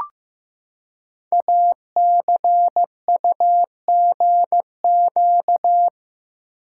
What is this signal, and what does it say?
Morse code, heard as a keyed tone, about 15 wpm, 700 Hz. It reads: ACUGQ
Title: Code Groups - 5 Letters - 15wpm